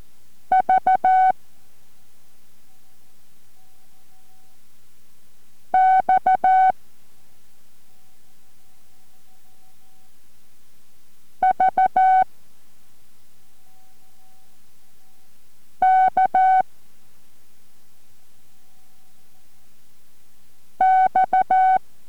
Morse code, heard as a keyed tone, about 14 wpm, 750 Hz. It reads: V X V K X